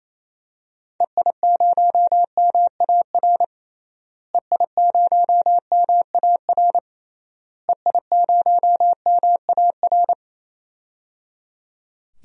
Morse code, heard as a keyed tone, about 28 wpm, 700 Hz. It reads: EI0MAR EI0MAR EI0MAR
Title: DXCC-100 Call Signs - 6 Characters - Repeated 3x - 28wpm